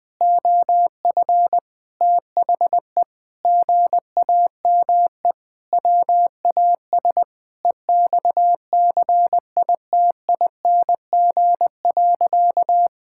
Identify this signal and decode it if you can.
Morse code, heard as a keyed tone, about 20 wpm, 700 Hz. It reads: OF THE GAME WAS EXCITING.
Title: Sentences with 5-Word Phrases - 20wpm